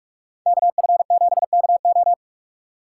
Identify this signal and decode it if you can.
Morse code, heard as a keyed tone, about 48 wpm, 700 Hz. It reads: KF7KO